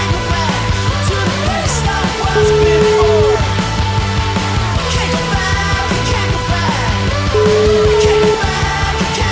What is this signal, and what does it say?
Ringback tone (Europe, 425 Hz), with noise.